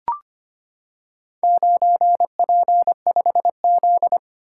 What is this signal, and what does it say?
Morse code, heard as a keyed tone, about 25 wpm, 700 Hz. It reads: 9P5Z
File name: practice / 25wpm / Four Letter-Number Combinations - 25wpm